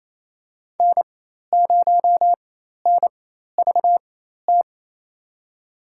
Morse code, heard as a keyed tone, about 28 wpm, spaced as for 13 wpm, 700 Hz. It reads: N0NVT
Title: USA Call Signs - 5 Characters - 4x Character Spacing - 28wpm